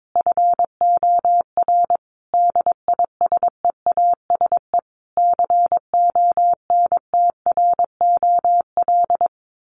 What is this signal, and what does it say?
Morse code, heard as a keyed tone, about 22 wpm, 700 Hz. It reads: FOR DISEASE CONTROL
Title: Sets of 5 Words Encore2 22wpm